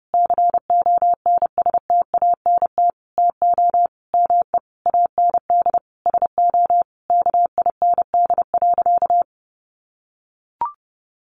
Morse code, heard as a keyed tone, about 30 wpm, 700 Hz. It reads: CONSTANT TO ME AND SO KIND.
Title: Sentences from Top 1500 Words 30wpm